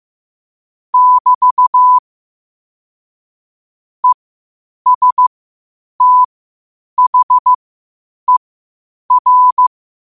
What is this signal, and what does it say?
Morse code, heard as a keyed tone, about 15 wpm, spaced as for 8 wpm, 1000 Hz. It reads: <BT> ESTHER
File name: qotdfiles / 8wpm